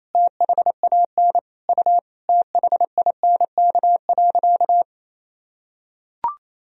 Morse code, heard as a keyed tone, about 28 wpm, 700 Hz. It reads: THAN U THINK.
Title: Sentences from Top 400 Words and Abbreviations 28wpm